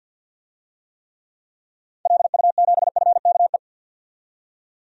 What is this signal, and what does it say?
Morse code, heard as a keyed tone, about 52 wpm, 700 Hz. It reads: LU7FKE
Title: DXCC-100 Call Signs - 6 Characters - 50wpm